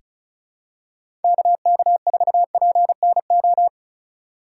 Morse code, heard as a keyed tone, about 34 wpm, 700 Hz. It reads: KK4PNO
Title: USA Call Signs - 6 Characters - 2x Character Spacing - 35wpm